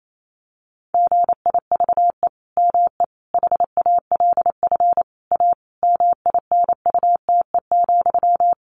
Morse code, heard as a keyed tone, about 28 wpm, 700 Hz. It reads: GIVE ME HALF A MINUTE,
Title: Sentences from Top 1500 Words 28wpm